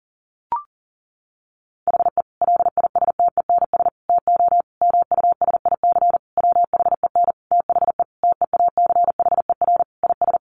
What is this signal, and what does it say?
Morse code, heard as a keyed tone, about 37 wpm, 700 Hz. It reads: HE LISTENS TO MUSIC WHEN THE TEACHER IS
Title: Additional Easy Phrases 40wpm